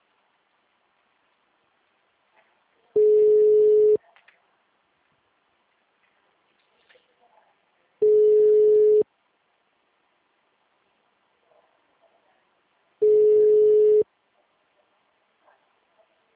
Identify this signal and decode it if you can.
Ringback tone (Europe, 425 Hz), with speech.
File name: out- -2031- - - .14134